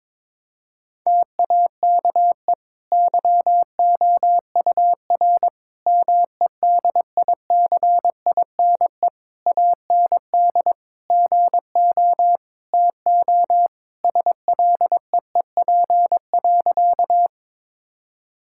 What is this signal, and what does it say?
Morse code, heard as a keyed tone, about 21 wpm, 700 Hz. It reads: TAKE YOUR MEDICINE AND GO TO SLEEP.
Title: Sentences from Top 1000 Words - Review Part 6 - 22wpm